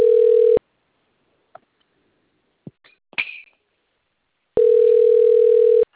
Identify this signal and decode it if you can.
Ringback tone (North America), 440 + 480 Hz.